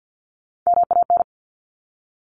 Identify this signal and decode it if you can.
Morse code, heard as a keyed tone, about 50 wpm, 700 Hz. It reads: MAN
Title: Top Three Letter Combinations in Words 50wpm